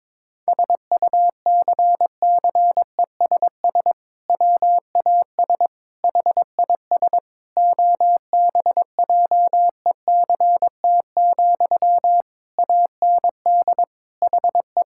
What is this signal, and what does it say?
Morse code, heard as a keyed tone, about 22 wpm, 700 Hz. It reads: SUCCESS WAS HIS OBJECT, AND HE
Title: Sentences from Top 1000 Words - Review Part 6 - 22wpm